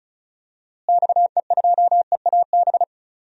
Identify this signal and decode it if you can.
Morse code, heard as a keyed tone, about 36 wpm, 700 Hz. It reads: XE2EAB
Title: DXCC-100 Call Signs - 6 Characters - Repeated 3x - 35wpm